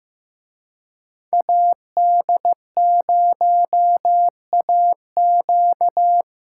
Morse code, heard as a keyed tone, about 18 wpm, 700 Hz. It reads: AD0AQ
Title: Speed Racing - USA 2x2 Call Signs 15wpm